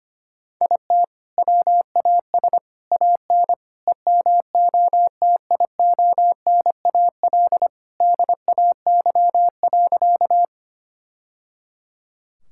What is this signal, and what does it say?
Morse code, heard as a keyed tone, about 25 wpm, 700 Hz. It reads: IT WAS AN EMOTIONAL DAY.
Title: Sentences from Top 1000 Words - Review Part 5 - 25wpm